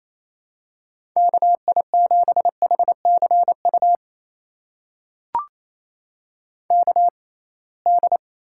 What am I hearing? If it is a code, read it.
Morse code, heard as a keyed tone, about 28 wpm, spaced as for 21 wpm, 700 Hz. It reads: KI7HCU KD
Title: USA Call Signs - 6 Characters - 6x Character Spacing - 28wpm